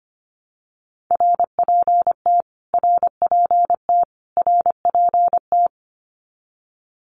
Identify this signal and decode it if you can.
Morse code, heard as a keyed tone, about 25 wpm, 700 Hz. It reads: RPT RPT RPT
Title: Abbreviations Repeated 3x - 25wpm